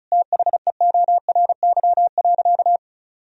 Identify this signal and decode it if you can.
Morse code, heard as a keyed tone, about 34 wpm, 700 Hz. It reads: THEORY.
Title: Sentences from Top 1000 Words - Review Part 4 - 35wpm